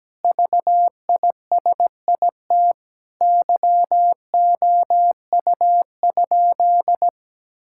Morse code, heard as a keyed tone, about 17 wpm, 700 Hz. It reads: VISIT YOU?
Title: Sentences from Top 1000 Words - Review Part 6 - 17wpm